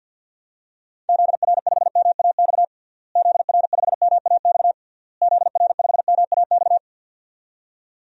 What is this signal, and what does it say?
Morse code, heard as a keyed tone, about 51 wpm, 700 Hz. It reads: ZR5MAX ZR5MAX ZR5MAX
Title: DXCC-100 Call Signs - 6 Characters - Repeated 3x - 50wpm